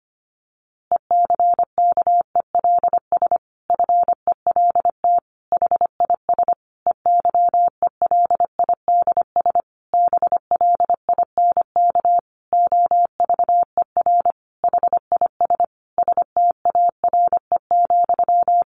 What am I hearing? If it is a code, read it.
Morse code, heard as a keyed tone, about 25 wpm, 700 Hz. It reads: ECKELS FELT HIS EYELIDS BLINK OVER HIS STARE,
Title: A Sound Of Thunder 25wpm